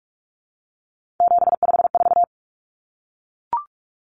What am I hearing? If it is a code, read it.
Morse code, heard as a keyed tone, about 44 wpm, 700 Hz. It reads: 754
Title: Common Cut Numbers - 3-Digit 45wpm